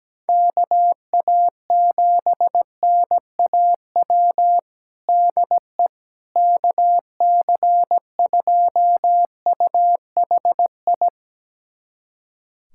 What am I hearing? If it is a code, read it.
Morse code, heard as a keyed tone, about 17 wpm, 700 Hz. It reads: KA7NAW DE KC2UHI
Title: USA Callsign DE Callsign 17wpm